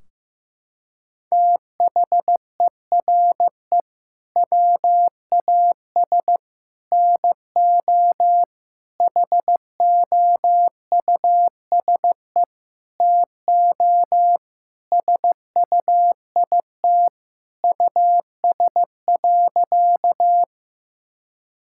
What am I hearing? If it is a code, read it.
Morse code, heard as a keyed tone, about 16 wpm, 700 Hz. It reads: THERE WAS NO HOUSE TO SUIT US.
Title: Sentences from Top 1600 Words 15wpm